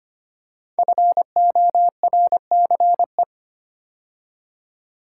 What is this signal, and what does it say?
Morse code, heard as a keyed tone, about 25 wpm, 700 Hz. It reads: FORCE
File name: 5 Letter Words - First 3 Letters Given - 25wpm